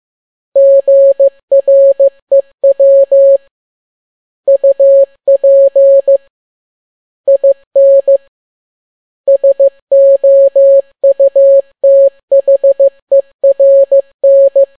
Morse code, heard as a keyed tone, about 15 wpm, 550 Hz. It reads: GREW UP IN SOUTHERN
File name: n6tt p1 15C13